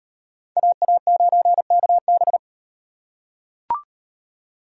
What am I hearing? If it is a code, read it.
Morse code, heard as a keyed tone, about 34 wpm, 700 Hz. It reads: AA9KB